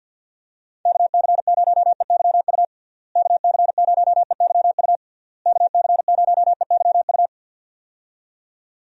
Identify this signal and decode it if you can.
Morse code, heard as a keyed tone, about 46 wpm, 700 Hz. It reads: KC0EYU KC0EYU KC0EYU
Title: USA Call Signs - 6 Characters - Repeated 3x - 50wpm